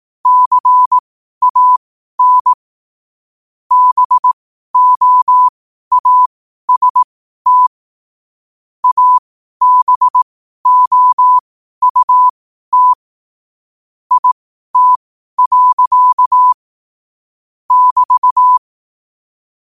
Morse code, heard as a keyed tone, about 18 wpm, spaced as for 12 wpm, 1000 Hz. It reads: CAN BOAST ABOUT IT. <BT>
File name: qotdfiles / 12wpm